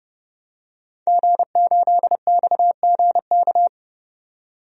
Morse code, heard as a keyed tone, about 31 wpm, 700 Hz. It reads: G8XGK
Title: DXCC-100 Call Signs - 5 Characters - 30wpm